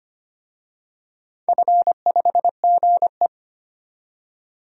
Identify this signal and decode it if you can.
Morse code, heard as a keyed tone, about 25 wpm, 690 Hz. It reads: F5GE
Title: DXCC-100 Call Signs - 4 Characters - 25wpm